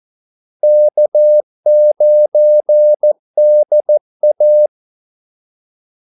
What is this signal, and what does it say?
Morse code, heard as a keyed tone, about 14 wpm, 600 Hz. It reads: K9DA
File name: Callsigns-14wpm